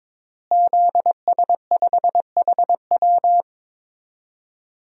Morse code, heard as a keyed tone, about 22 wpm, 700 Hz. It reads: ZS5HW
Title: DXCC-100 Call Signs - 5 Characters - Repeated 3x - 22wpm